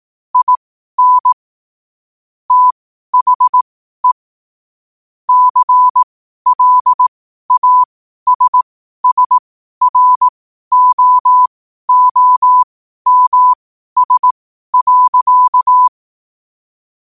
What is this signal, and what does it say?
Morse code, heard as a keyed tone, about 18 wpm, spaced as for 12 wpm, 1000 Hz. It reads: IN THE CLASSROOMS.
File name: qotdfiles / 12wpm